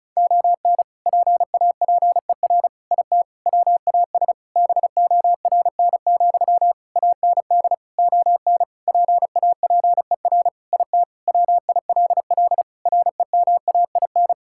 Morse code, heard as a keyed tone, about 36 wpm, 700 Hz. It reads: ON PAPER IT WAS BORN, AND ON PAPER IT WILL REMAIN
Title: Sentences from Top 1600 Words 35wpm